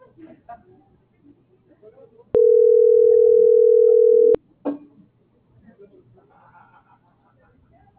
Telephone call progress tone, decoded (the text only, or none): ringback tone